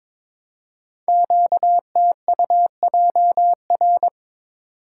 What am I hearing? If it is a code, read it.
Morse code, heard as a keyed tone, about 22 wpm, 700 Hz. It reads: QTUJR
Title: Code Groups - 5 Letters - 22wpm